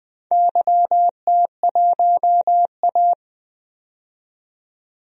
Morse code, heard as a keyed tone, about 21 wpm, 700 Hz. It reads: YT1A
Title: DXCC-100 Call Signs - 4 Characters - 20wpm